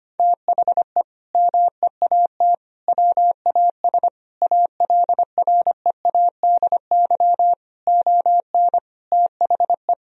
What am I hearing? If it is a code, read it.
Morse code, heard as a keyed tone, about 25 wpm, 700 Hz. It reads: THE MEAT WAS ALREADY ON THE
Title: Sentences from Top 1000 Words - Review Part 3 - 25wpm